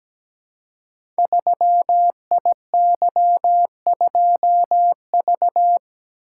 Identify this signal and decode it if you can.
Morse code, heard as a keyed tone, about 17 wpm, 700 Hz. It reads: 3IY2V
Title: Code Groups - 5 Letter:Numbers - 17wpm